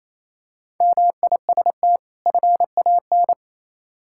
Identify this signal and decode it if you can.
Morse code, heard as a keyed tone, about 27 wpm, 690 Hz. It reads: MIST FAN
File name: Easy Words - Speed-Racing with Hints - 22wpm